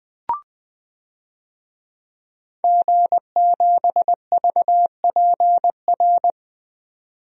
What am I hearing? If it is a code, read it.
Morse code, heard as a keyed tone, about 20 wpm, 700 Hz. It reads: G7VPR
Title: DXCC-100 Call Signs - 5 Characters - 20wpm